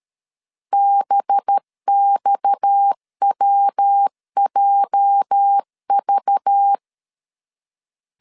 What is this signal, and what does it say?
Morse code, heard as a keyed tone, about 13 wpm, 800 Hz. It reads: BXWJV
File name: LEZIONI / 8 YJ45 / 8 YJ 12wpm 2235